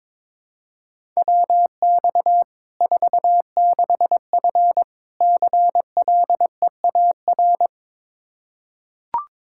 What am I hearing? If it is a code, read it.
Morse code, heard as a keyed tone, about 22 wpm, 700 Hz. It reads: WX 46F CLEAR
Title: Practice QSO - Exchange 2 - Info - 22wpm